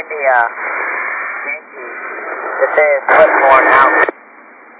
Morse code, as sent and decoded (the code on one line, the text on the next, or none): none